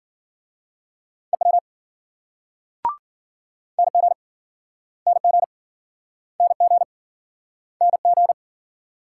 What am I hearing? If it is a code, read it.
Morse code, heard as a keyed tone, about 54 wpm, 700 Hz. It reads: EW NG NG NG NG